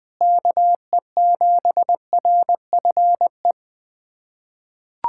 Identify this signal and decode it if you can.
Morse code, heard as a keyed tone, about 20 wpm, 700 Hz. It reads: KE7RFE
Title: USA Callsign DE Callsign - 6 Characters - 20wpm